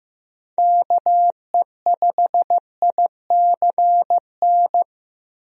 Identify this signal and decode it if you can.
Morse code, heard as a keyed tone, about 15 wpm, 700 Hz. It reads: KE5ICN
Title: USA Call Signs - 6 Characters - Repeated 2x - 15wpm